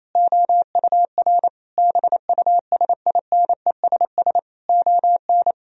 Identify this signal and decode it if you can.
Morse code, heard as a keyed tone, about 28 wpm, 700 Hz. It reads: OUR BUSINESS ON
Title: Sentences from Top 1000 Words - Review Part 4 - 28wpm